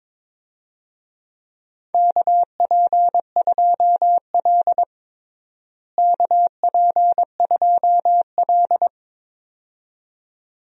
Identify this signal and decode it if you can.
Morse code, heard as a keyed tone, about 22 wpm, 700 Hz. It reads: KP2L KP2L
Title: DXCC-100 Call Signs - 4 Characters - Repeated 2x - 22wpm